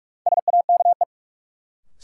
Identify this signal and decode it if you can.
Morse code, heard as a keyed tone, about 46 wpm, 700 Hz. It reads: SAKE
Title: Sets of 4 Words - New Word Drill - 45wpm